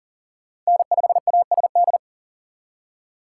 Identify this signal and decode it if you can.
Morse code, heard as a keyed tone, about 51 wpm, 700 Hz. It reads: N5ASD